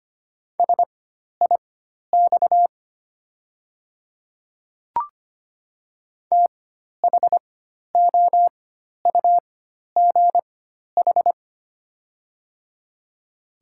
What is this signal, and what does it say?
Morse code, heard as a keyed tone, about 25 wpm, spaced as for 12 wpm, 700 Hz. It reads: SIX THOUGH
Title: 4x Character Spacing - Top 500 Words - 25wpm